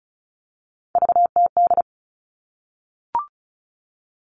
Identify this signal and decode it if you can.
Morse code, heard as a keyed tone, about 34 wpm, 700 Hz. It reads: VTD
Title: All Cut Numbers - 3-Digit 35wpm